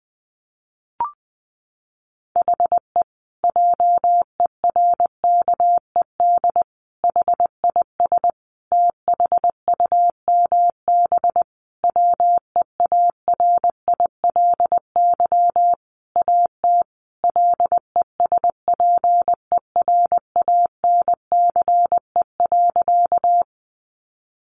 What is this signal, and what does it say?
Morse code, heard as a keyed tone, about 20 wpm, 700 Hz. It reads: HE JERKED HIS THUMB WEARILY AT LESPERANCE.
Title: A Sound Of Thunder 20wpm